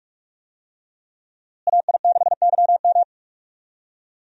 Morse code, heard as a keyed tone, about 41 wpm, 700 Hz. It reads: AI6YM
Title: USA Call Signs - 5 Characters - Repeated 3x - 45wpm